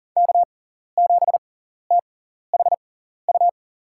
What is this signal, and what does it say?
Morse code, heard as a keyed tone, about 40 wpm, spaced as for 14 wpm, 700 Hz. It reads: K7THU
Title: USA Call Signs - 5 Characters - 6x Character Spacing - 40wpm